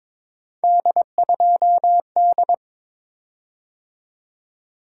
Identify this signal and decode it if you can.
Morse code, heard as a keyed tone, about 23 wpm, 700 Hz. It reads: D2D
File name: practice / 22wpm / Three Letter-Number Combinations - 22wpm